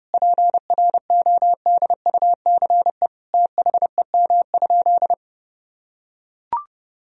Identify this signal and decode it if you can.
Morse code, heard as a keyed tone, about 30 wpm, 700 Hz. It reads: PRODUCE THEM?
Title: Sentences from Top 1000 Words - Review Part 3 - 30wpm